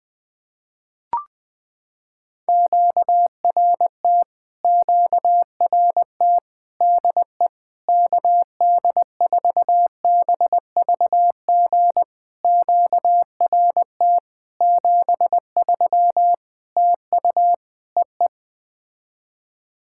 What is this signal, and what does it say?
Morse code, heard as a keyed tone, about 20 wpm, 700 Hz. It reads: QRT QRT DE KD4BVG QRT 73 TU EE
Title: POTA QSO Practice - 20wpm